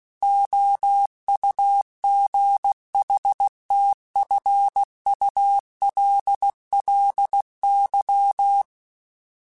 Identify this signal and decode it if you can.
Morse code, heard as a keyed tone, about 16 wpm, 800 Hz. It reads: OUGHTFULLY